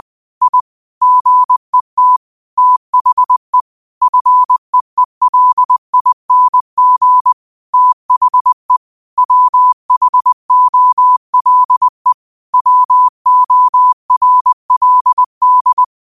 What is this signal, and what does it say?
Morse code, heard as a keyed tone, about 20 wpm, 1000 Hz. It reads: I GET THE FEELING THE WHOLE WORLD